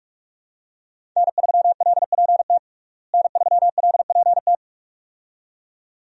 Morse code, heard as a keyed tone, about 46 wpm, 700 Hz. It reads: N3LPT N3LPT
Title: USA Call Signs - 5 Characters - Repeated 2x - 45wpm